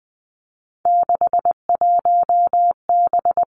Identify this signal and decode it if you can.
Morse code, heard as a keyed tone, about 20 wpm, 700 Hz. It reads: 61B
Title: Three Challenging Letter-Number Combinations 20wpm